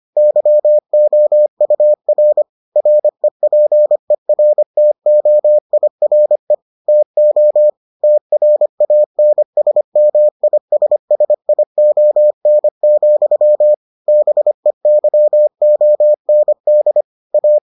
Morse code, heard as a keyed tone, about 25 wpm, 600 Hz. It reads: YOUR REPERTOIRE TO TRANSMISSION, BEYOND A